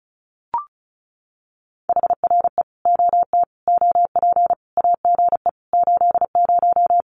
Morse code, heard as a keyed tone, about 34 wpm, 700 Hz. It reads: HRE OT OP AGE 80
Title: Common QSO Phrases 35wpm